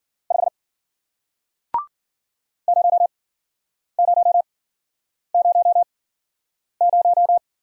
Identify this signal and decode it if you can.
Morse code, heard as a keyed tone, about 50 wpm, spaced as for 9 wpm, 700 Hz. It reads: 5 0000
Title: Speed-Racing Single Letter-Number 40wpm